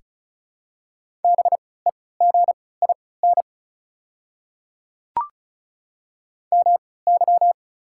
Morse code, heard as a keyed tone, about 36 wpm, spaced as for 20 wpm, 700 Hz. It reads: BEGIN MY
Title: 3x Character Spacing - Top 500 Words - 35wpm